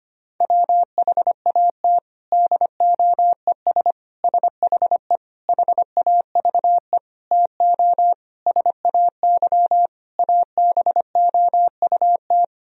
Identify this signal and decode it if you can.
Morse code, heard as a keyed tone, about 25 wpm, 700 Hz. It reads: WHAT DOES SHE HAVE TO SAY ABOUT
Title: Sentences from Top 1000 Words - Review Part 3 - 25wpm